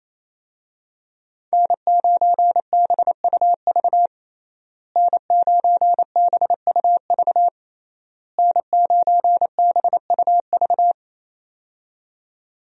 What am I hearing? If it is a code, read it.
Morse code, heard as a keyed tone, about 28 wpm, 700 Hz. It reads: N9BUV N9BUV N9BUV
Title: USA Call Signs - 5 Characters - Repeated 3x - 28wpm